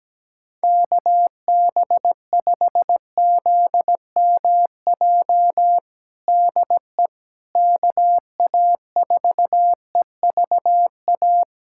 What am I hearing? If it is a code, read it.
Morse code, heard as a keyed tone, about 17 wpm, 700 Hz. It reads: KB5ZMJ DE KA4EVA
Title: USA Callsign DE Callsign - 6 Characters - 17wpm